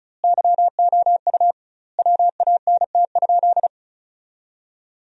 Morse code, heard as a keyed tone, about 36 wpm, 700 Hz. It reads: YOU WANT?
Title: 2x Word-Spacing - Sentences from Top 200 Words - 35wpm